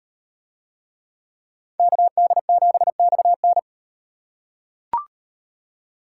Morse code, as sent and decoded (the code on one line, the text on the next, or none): -.- -.. --... -..- -.
KD7XN